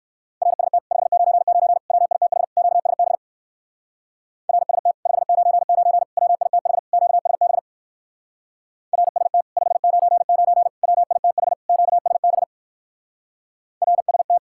Morse code, heard as a keyed tone, about 54 wpm, 700 Hz. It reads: RST 599 WITH QSB RST 599 WITH QSB RST 599 WITH QSB RST
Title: Speed-Racing Common QSO Phrases - 45wpm